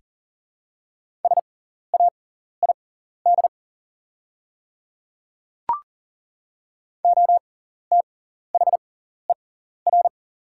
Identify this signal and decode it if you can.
Morse code, heard as a keyed tone, about 40 wpm, spaced as for 14 wpm, 700 Hz. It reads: SAID OTHER
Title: 6x Character Spacing - Top 100 Words - 40wpm